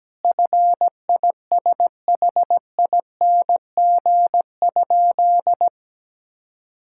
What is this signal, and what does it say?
Morse code, heard as a keyed tone, about 17 wpm, 700 Hz. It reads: FISHING?
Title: Sentences from Top 1000 Words - Review Part 5 - 17wpm